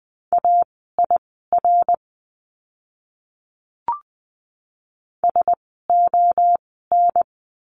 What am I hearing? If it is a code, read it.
Morse code, heard as a keyed tone, about 20 wpm, spaced as for 14 wpm, 700 Hz. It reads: AIR SON
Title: 101 Lesson 17f - Letter u - Words 20wpm